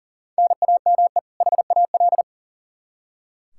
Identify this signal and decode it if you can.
Morse code, heard as a keyed tone, about 40 wpm, 700 Hz. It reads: NAME HAL
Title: Practice QSO - Exchange 1 - Beginning - 40wpm